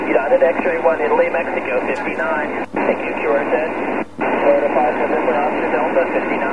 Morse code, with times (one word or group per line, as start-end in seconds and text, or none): none